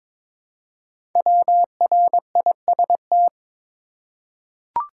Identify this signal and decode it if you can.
Morse code, heard as a keyed tone, about 22 wpm, 700 Hz. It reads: WRIST
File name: 5 Letter Words - First 3 Letters Given - 22wpm